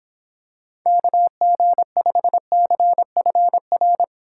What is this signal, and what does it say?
Morse code, heard as a keyed tone, about 26 wpm, 700 Hz. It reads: KG5CFR